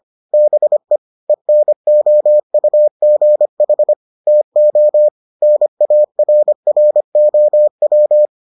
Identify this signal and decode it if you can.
Morse code, heard as a keyed tone, about 25 wpm, 600 Hz. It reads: BE ENOUGH TO NARROW